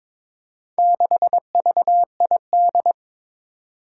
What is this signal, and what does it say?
Morse code, heard as a keyed tone, about 22 wpm, 700 Hz. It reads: 6VID
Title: Four Challenging Letter-Number Combinations 22wpm